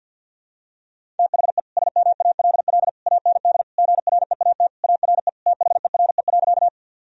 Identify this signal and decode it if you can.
Morse code, heard as a keyed tone, about 46 wpm, 700 Hz. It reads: THE SMALL AND GREAT ARE THERE.
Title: Sentences from Top 1000 Words - Review Part 1 - 50wpm